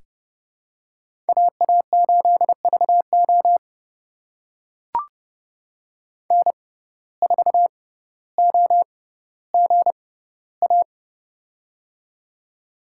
Morse code, heard as a keyed tone, about 30 wpm, spaced as for 15 wpm, 700 Hz. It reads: AA8VO N4OGA
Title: USA Call Signs - 5 Characters - 6x Character Spacing - 30wpm